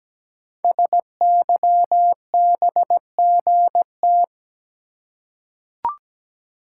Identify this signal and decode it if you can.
Morse code, heard as a keyed tone, about 17 wpm, 700 Hz. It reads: SYBGT
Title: Code Groups - 5 Letters - 17wpm